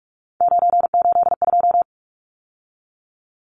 Morse code, heard as a keyed tone, about 44 wpm, 700 Hz. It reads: 982
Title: Numbers - 3-Digit 45wpm